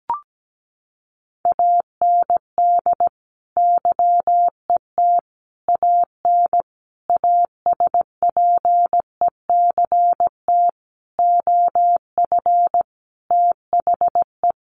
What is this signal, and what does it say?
Morse code, heard as a keyed tone, about 17 wpm, 700 Hz. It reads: AND YET AN ASPECT OF THE
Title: Sentences from Top 1300 Words 17wpm